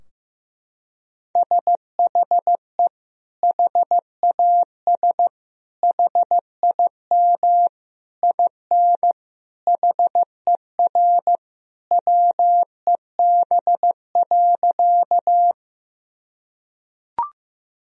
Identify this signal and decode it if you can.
Morse code, heard as a keyed tone, about 15 wpm, 700 Hz. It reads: SHE HAS HIM IN HER WEB.
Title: Sentences from Top 1000 Words - Review Part 4 - 15wpm